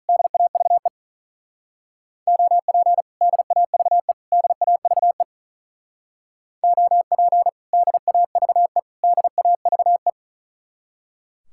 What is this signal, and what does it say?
Morse code, heard as a keyed tone, about 45 wpm, 700 Hz. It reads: DAVE OP DAVE DAVE OP DAVE DAVE